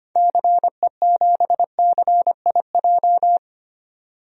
Morse code, heard as a keyed tone, about 25 wpm, 700 Hz. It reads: CE7CIJ
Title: DXCC-100 Call Signs - 6 Characters - 25wpm